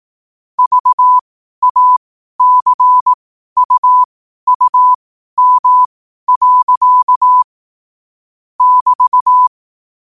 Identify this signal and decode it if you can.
Morse code, heard as a keyed tone, about 18 wpm, spaced as for 12 wpm, 1000 Hz. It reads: VACUUM. <BT>
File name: qotdfiles / 12wpm